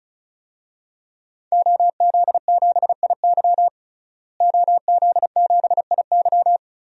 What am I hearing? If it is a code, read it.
Morse code, heard as a keyed tone, about 36 wpm, 700 Hz. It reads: OZ7IY OZ7IY
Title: DXCC-100 Call Signs - 5 Characters - Repeated 3x - 35wpm